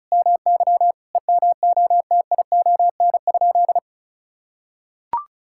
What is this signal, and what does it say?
Morse code, heard as a keyed tone, about 34 wpm, 700 Hz. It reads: MY EMOTION?
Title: Sentences from Top 1600 Words 35wpm